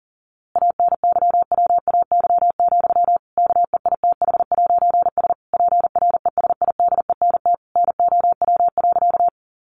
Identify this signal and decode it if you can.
Morse code, heard as a keyed tone, about 40 wpm, 700 Hz. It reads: ANYWAY, KEITH'S PRESIDENT NOW.
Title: A Sound Of Thunder 40wpm